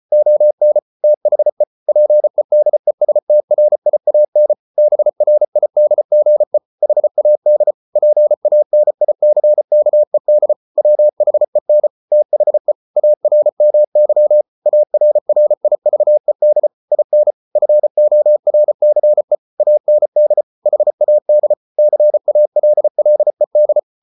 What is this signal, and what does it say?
Morse code, heard as a keyed tone, about 34 wpm, 600 Hz. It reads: ON THE PEDESTRIAN BRIDGE HAD PANICKED WHEN THE ARMY ARRIVED IN FORCE AND HAD CALLED